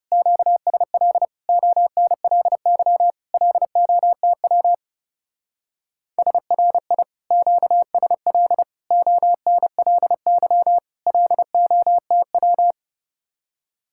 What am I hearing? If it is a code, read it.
Morse code, heard as a keyed tone, about 35 wpm, 700 Hz. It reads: QSL ONLY LOTW SRI QSL ONLY LOTW
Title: Speed-Racing Common QSO Phrases - 30wpm